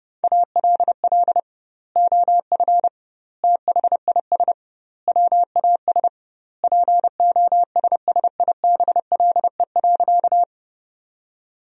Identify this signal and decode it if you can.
Morse code, heard as a keyed tone, about 30 wpm, 700 Hz. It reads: ALL OF THIS WAS POSSIBLE.
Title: 2x Word-Spacing - Sentences from Top 500 Words - 30wpm